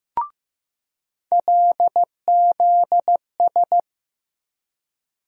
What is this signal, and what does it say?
Morse code, heard as a keyed tone, about 15 wpm, 700 Hz. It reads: LZS
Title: USA Call Sign Postfixes - 3 Letters 15wpm